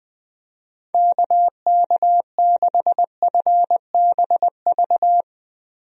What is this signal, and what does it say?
Morse code, heard as a keyed tone, about 20 wpm, 700 Hz. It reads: KK6FBV
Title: USA Call Signs - 6 Characters - Repeated 3x - 20wpm